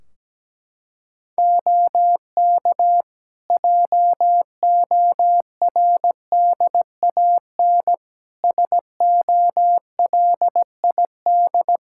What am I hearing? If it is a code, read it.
Morse code, heard as a keyed tone, about 17 wpm, 700 Hz. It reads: OK JORDAN SOLID
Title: Practice QSO - Exchange 3 - More Info - 17wpm